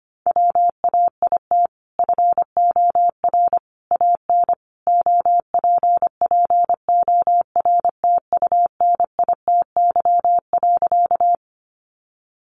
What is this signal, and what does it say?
Morse code, heard as a keyed tone, about 25 wpm, 700 Hz. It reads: WAIT FOR AN OPPORTUNITY.
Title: Sentences from Top 1000 Words 25wpm